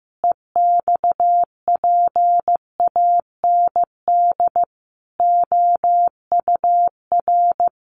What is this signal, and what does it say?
Morse code, heard as a keyed tone, about 15 wpm, 700 Hz. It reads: EXPAND OUR
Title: Sentences from Top 1400 Words 15wpm